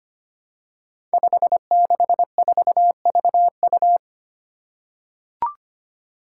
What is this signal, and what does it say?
Morse code, heard as a keyed tone, about 25 wpm, 700 Hz. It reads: 564VU
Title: Five Challenging Letter-Number Combinations 25wpm